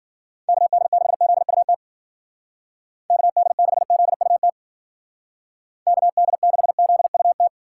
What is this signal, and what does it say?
Morse code, heard as a keyed tone, about 49 wpm, 700 Hz. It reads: KD6ZUT KD6ZUT KD6ZUT
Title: Speed Racing - USA 2x3 Call Signs 40wpm